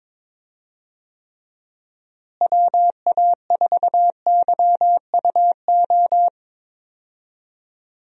Morse code, heard as a keyed tone, about 22 wpm, 700 Hz. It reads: WA4YUO
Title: USA Call Signs - 6 Characters - Repeated 2x - 22wpm